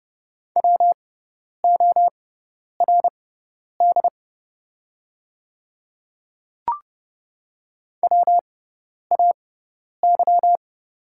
Morse code, heard as a keyed tone, about 31 wpm, spaced as for 10 wpm, 700 Hz. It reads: WORD WAY
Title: 6x Character Spacing - Top 200 Words - 30wpm